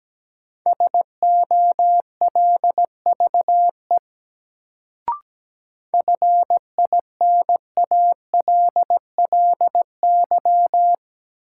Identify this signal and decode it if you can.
Morse code, heard as a keyed tone, about 17 wpm, 700 Hz. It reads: SOLVE FINALLY
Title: Sentences from Top 1000 Words - New Word Drill - 17wpm